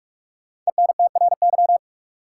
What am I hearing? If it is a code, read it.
Morse code, heard as a keyed tone, about 42 wpm, 700 Hz. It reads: ENTRY
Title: Sentences from Top 1500 Words - New Word Drill - 45wpm